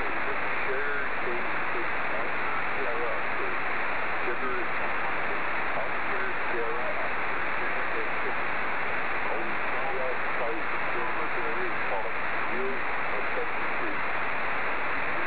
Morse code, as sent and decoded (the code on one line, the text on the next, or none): none